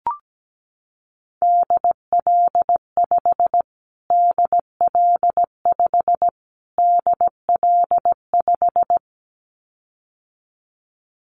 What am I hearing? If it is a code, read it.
Morse code, heard as a keyed tone, about 17 wpm, 700 Hz. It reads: DL5 DL5 DL5
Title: Callsign Prefixes for Top 25 DXCC - Repeated 3x - 17wpm